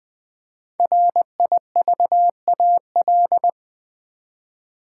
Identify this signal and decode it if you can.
Morse code, heard as a keyed tone, about 20 wpm, 700 Hz. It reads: RIVAL
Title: 5 Letter Words - First 3 Letters Given - 20wpm